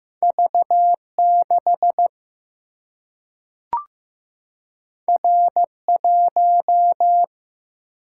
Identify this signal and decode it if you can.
Morse code, heard as a keyed tone, about 15 wpm, 700 Hz. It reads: V 6 R 1
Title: Single-Letter Single-Number 15wpm